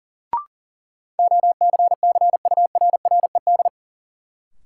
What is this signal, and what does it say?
Morse code, heard as a keyed tone, about 40 wpm, 700 Hz. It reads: OCCURRED
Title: Sets of 5 Words Encore - New Word Drill - 40wpm